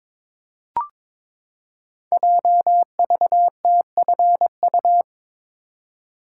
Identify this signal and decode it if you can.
Morse code, heard as a keyed tone, about 22 wpm, 700 Hz. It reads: JVTFU
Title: Code Groups - 5 Letter:Numbers - 22wpm